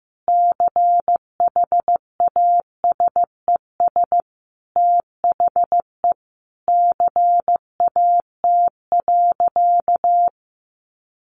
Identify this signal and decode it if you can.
Morse code, heard as a keyed tone, about 15 wpm, 700 Hz. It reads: CHASES THE CAT.
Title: Easy Phrases 15wpm